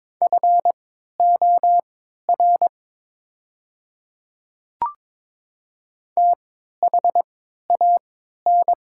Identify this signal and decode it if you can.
Morse code, heard as a keyed tone, about 22 wpm, spaced as for 13 wpm, 690 Hz. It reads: FOR THAN
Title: 3x Character Spacing - Top 200 Words - 22wpm